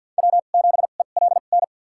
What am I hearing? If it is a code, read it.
Morse code, heard as a keyed tone, about 48 wpm, spaced as for 35 wpm, 700 Hz. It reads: W7ELN
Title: USA Call Signs - 5 Characters - 2x Character Spacing - 50wpm